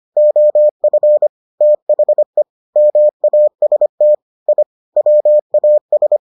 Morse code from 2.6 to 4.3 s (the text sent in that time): MAST